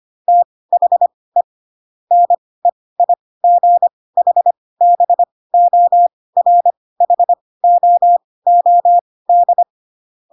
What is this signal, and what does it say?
Morse code, heard as a keyed tone, about 25 wpm, spaced as for 18 wpm, 700 Hz. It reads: THE NEIGHBORHOOD
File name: Pinocchio / Pinocchio ch2 12 25 ik1hge